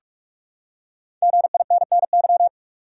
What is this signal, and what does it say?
Morse code, heard as a keyed tone, about 46 wpm, 700 Hz. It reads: GINNY
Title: USA Common Names - 45wpm